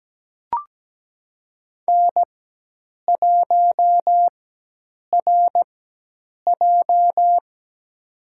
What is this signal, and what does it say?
Morse code, heard as a keyed tone, about 17 wpm, spaced as for 8 wpm, 700 Hz. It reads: N1RJ
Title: USA Call Signs - 4x Character Spacing - 17wpm